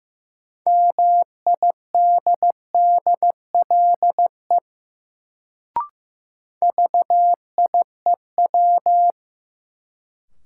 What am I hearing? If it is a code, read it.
Morse code, heard as a keyed tone, about 15 wpm, 700 Hz. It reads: MIDDLE VIEW
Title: Sentences from Top 900 Words - New Word Drill - 15wpm